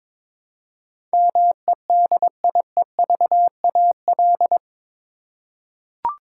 Morse code, heard as a keyed tone, about 22 wpm, 700 Hz. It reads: MEDIEVAL
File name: practice / 22wpm / Eight Letter Words - 22wpm